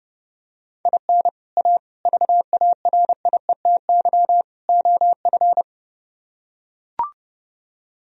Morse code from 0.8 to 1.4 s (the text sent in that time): IN